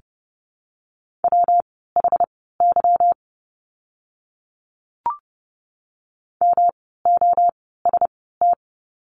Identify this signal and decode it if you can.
Morse code, heard as a keyed tone, about 30 wpm, spaced as for 17 wpm, 700 Hz. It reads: WHY MOST
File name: course / 30:10wpm / 237 Lesson 38f - Letter q - Words 30wpm